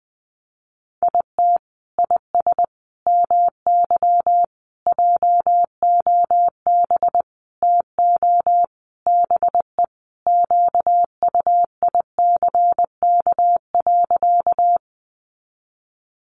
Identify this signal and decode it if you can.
Morse code, heard as a keyed tone, about 20 wpm, 700 Hz. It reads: IT IS MY JOB TO BE QUICK.
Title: Sentences from Top 600 Words 20wpm